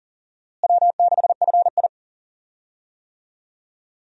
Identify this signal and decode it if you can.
Morse code, heard as a keyed tone, about 40 wpm, 700 Hz. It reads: W6FI
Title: USA Call Signs - 4 Characters - Repeated 3x - 40wpm